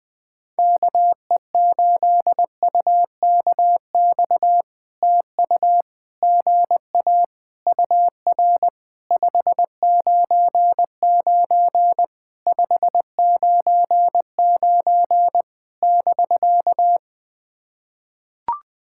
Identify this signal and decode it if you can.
Morse code, heard as a keyed tone, about 20 wpm, 700 Hz. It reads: KE8UKX TU GA UR 599 599 <BK>
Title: POTA QSO Practice - 20wpm